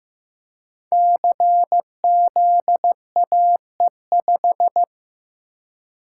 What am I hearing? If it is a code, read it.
Morse code, heard as a keyed tone, about 15 wpm, 700 Hz. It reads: CZAE5
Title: Code Groups - 5 Letter:Numbers - 15wpm